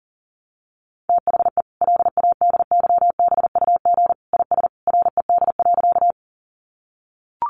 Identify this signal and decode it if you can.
Morse code, heard as a keyed tone, about 40 wpm, 700 Hz. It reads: THE LADYBUG IS RED.
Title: Additional Easy Phrases 40wpm